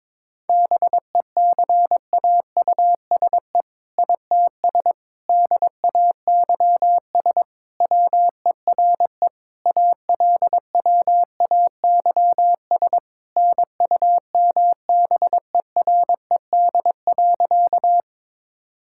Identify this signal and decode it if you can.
Morse code, heard as a keyed tone, about 22 wpm, 690 Hz. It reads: BECAUSE ITS DAYS WERE ALWAYS NUMBERED.
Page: morse-code-ninja-practice-22wpm